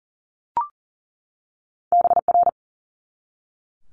Morse code, heard as a keyed tone, about 39 wpm, 700 Hz. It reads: BR
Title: Multi-letter Phonemes 40wpm